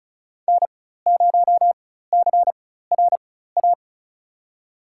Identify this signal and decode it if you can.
Morse code, heard as a keyed tone, about 36 wpm, spaced as for 16 wpm, 700 Hz. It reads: N0CRA
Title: USA Call Signs - 4x Character Spacing - 35wpm